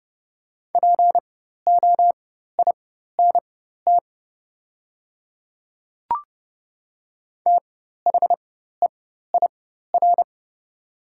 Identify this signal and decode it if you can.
Morse code, heard as a keyed tone, about 30 wpm, spaced as for 14 wpm, 690 Hz. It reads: POINT THEIR
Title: 4x Character Spacing - Top 500 Words - 30wpm